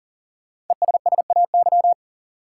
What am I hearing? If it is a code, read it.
Morse code, heard as a keyed tone, about 40 wpm, 700 Hz. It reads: ESSAY
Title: Sentences from Top 1500 Words - New Word Drill - 40wpm